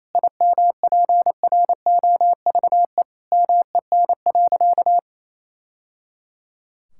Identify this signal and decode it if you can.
Morse code, heard as a keyed tone, about 28 wpm, 700 Hz. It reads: IMPROVE MEN.
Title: Sentences from Top 1000 Words - Review Part 5 - 28wpm